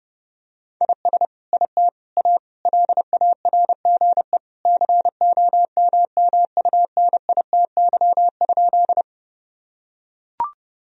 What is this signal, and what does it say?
Morse code, heard as a keyed tone, about 30 wpm, 700 Hz. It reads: IS IT A LARGE COMMUNITY?
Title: Sentences from Top 1000 Words - Review Part 4 - 30wpm